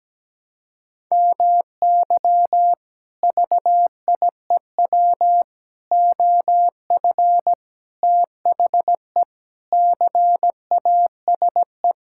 Morse code, heard as a keyed tone, about 17 wpm, 700 Hz. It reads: MY VIEW OF THE CASE
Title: Sentences from Top 1000 Words - Review Part 6 - 17wpm